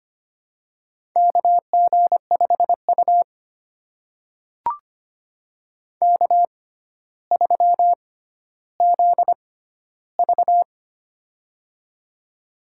Morse code, heard as a keyed tone, about 25 wpm, spaced as for 13 wpm, 700 Hz. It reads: KG5U K3ZV
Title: USA Call Signs - 6x Character Spacing - 25wpm